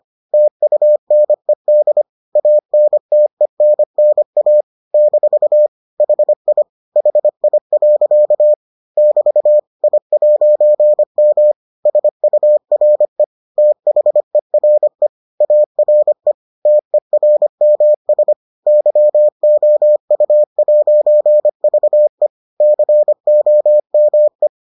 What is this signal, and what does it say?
Morse code, heard as a keyed tone, about 25 wpm, 600 Hz. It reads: TUNED ANTENNA - HI HI. <BT> I'M SURE THERE ARE TERMS YOU'VE COME